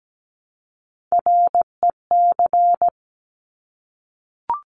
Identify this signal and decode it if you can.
Morse code, heard as a keyed tone, about 17 wpm, 700 Hz. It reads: REC
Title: Top Three Letter Combinations in Words 17wpm